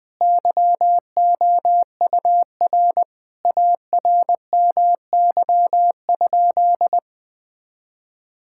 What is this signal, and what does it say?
Morse code, heard as a keyed tone, about 20 wpm, 700 Hz. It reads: YOUR ARMY?
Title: Sentences from Top 1000 Words - Review Part 5 - 20wpm